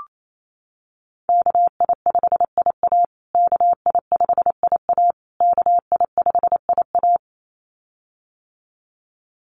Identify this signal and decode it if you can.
Morse code, heard as a keyed tone, about 28 wpm, 700 Hz. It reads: KI5IA KI5IA KI5IA
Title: USA Call Signs Repeated 3x 28wpm